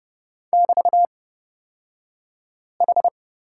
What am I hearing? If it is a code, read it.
Morse code, heard as a keyed tone, about 30 wpm, spaced as for 5 wpm, 690 Hz. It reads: <BT>H
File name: Lesson 36d - BT - Two Characters 30wpm